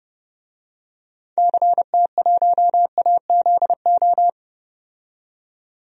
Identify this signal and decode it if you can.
Morse code, heard as a keyed tone, about 30 wpm, 700 Hz. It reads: CT1AZO
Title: DXCC-100 Call Signs - 6 Characters - 30wpm